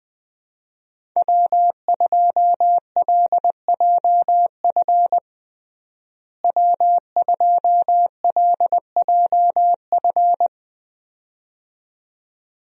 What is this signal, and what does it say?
Morse code, heard as a keyed tone, about 20 wpm, 700 Hz. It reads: W2LJF W2LJF
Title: USA Call Signs - 5 Characters - Repeated 2x - 20wpm